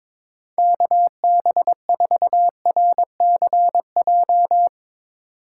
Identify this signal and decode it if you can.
Morse code, heard as a keyed tone, about 22 wpm, 700 Hz. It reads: KB4RCJ
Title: USA Call Signs - 4x Character Spacing - 22wpm